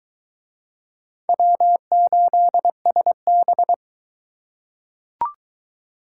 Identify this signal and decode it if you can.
Morse code, heard as a keyed tone, about 22 wpm, 700 Hz. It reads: W8SB